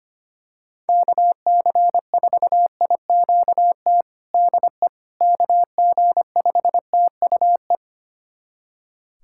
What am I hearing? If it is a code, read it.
Morse code, heard as a keyed tone, about 25 wpm, 700 Hz. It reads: KC4IQT DE KG5TUE
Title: USA Callsign DE Callsign 25wpm